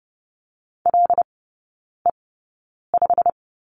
Morse code, heard as a keyed tone, about 30 wpm, spaced as for 9 wpm, 700 Hz. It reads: LE5
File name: Lesson 15e - Number 5 - Three Characters 30wpm